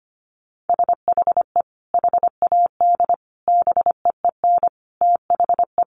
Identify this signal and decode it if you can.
Morse code, heard as a keyed tone, about 25 wpm, 700 Hz. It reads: SHE HAD BEEN THE
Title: Sentences from Top 1200 Words 25wpm